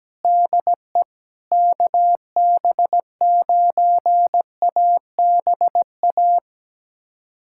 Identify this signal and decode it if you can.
Morse code, heard as a keyed tone, about 17 wpm, 700 Hz. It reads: DE KB9ABA
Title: USA Callsign DE Callsign - 6 Characters - 17wpm